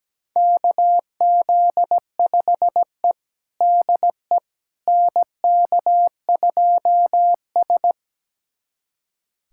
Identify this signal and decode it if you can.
Morse code, heard as a keyed tone, about 17 wpm, 700 Hz. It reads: KZ5E DE NK2S
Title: USA Callsign DE Callsign - 4 Characters - 17wpm